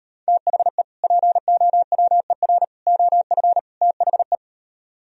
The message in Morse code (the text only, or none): THE POWER OF THE